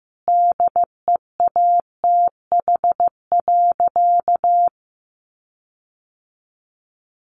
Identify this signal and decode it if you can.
Morse code, heard as a keyed tone, about 15 wpm, 700 Hz. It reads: DEATH.
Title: Sentences from Top 1300 Words 15wpm